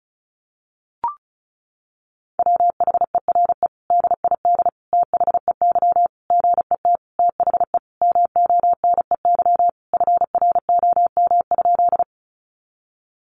Morse code, heard as a keyed tone, about 34 wpm, 700 Hz. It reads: WHERE DID THEY GET THE MONEY FROM?
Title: Sentences from Top 400 Words 35wpm